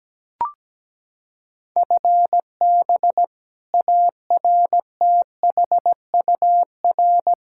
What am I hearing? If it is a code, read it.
Morse code, heard as a keyed tone, about 17 wpm, 700 Hz. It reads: FB ARTHUR
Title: Practice QSO - Exchange 3 - More Info - 17wpm